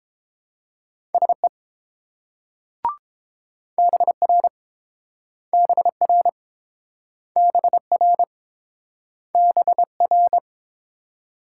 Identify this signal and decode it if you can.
Morse code, heard as a keyed tone, about 30 wpm, 700 Hz. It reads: SE BR BR BR BR